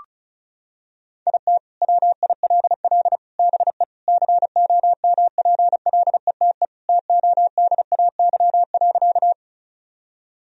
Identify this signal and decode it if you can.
Morse code, heard as a keyed tone, about 34 wpm, 700 Hz. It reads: IT WILL BE COMPLETE TODAY.
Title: Sentences from Top 1000 Words - Review Part 2 - 35wpm